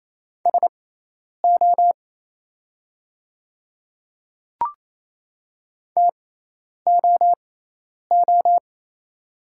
Morse code, heard as a keyed tone, about 28 wpm, spaced as for 10 wpm, 690 Hz. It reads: SO TOO
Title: 6x Character Spacing - Top 200 Words - 28wpm